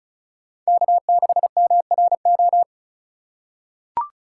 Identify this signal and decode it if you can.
Morse code, heard as a keyed tone, about 36 wpm, 690 Hz. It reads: K6MRO
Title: USA Call Signs - 5 Characters - 3x Character Spacing - 35wpm